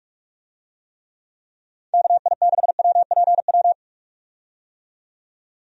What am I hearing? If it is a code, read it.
Morse code, heard as a keyed tone, about 46 wpm, 700 Hz. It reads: KI6WPW
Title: USA Call Signs - 6 Characters - 45wpm